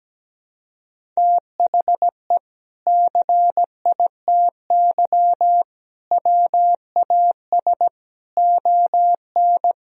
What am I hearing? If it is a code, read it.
Morse code, heard as a keyed tone, about 17 wpm, 700 Hz. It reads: THE CITY WAS ON
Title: Sentences from Top 1000 Words - Review Part 1 - 17wpm